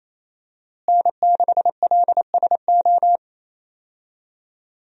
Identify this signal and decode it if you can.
Morse code, heard as a keyed tone, about 27 wpm, 700 Hz. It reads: N6LSO
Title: USA Call Signs - 5 Characters - Repeated 2x - 28wpm